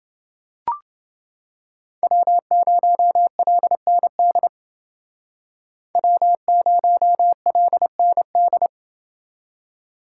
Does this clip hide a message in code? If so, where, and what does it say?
Yes: Morse code from 2.0 to 8.7 s, heard as a keyed tone, about 29 wpm, 700 Hz. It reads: W0LND W0LND